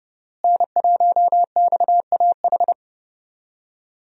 Morse code, heard as a keyed tone, about 30 wpm, 700 Hz. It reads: N1XAH
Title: USA Call Signs - 5 Characters - Repeated 2x - 30wpm